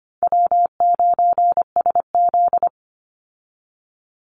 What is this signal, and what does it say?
Morse code, heard as a keyed tone, about 25 wpm, 700 Hz. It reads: W9SZ
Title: Easy Call Signs Repeated 3x - 25wpm